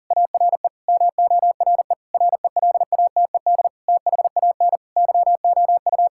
Morse code, heard as a keyed tone, about 37 wpm, 700 Hz. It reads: ARE MORE RELATED THAN YOU
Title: Sentences from Top 1000 Words - Review Part 5 - 40wpm